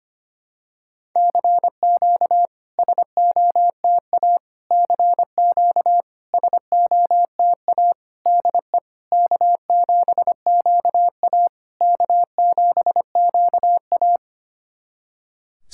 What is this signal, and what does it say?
Morse code, heard as a keyed tone, about 25 wpm, 700 Hz. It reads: CQ SOTA CQ SOTA DE K7QA K7QA
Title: SOTA QSO Practice 25wpm